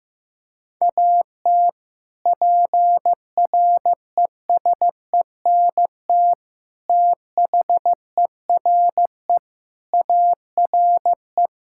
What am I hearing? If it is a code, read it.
Morse code, heard as a keyed tone, about 15 wpm, 700 Hz. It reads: AT PRESENT THERE ARE
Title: Sentences from Top 1000 Words - Review Part 5 - 15wpm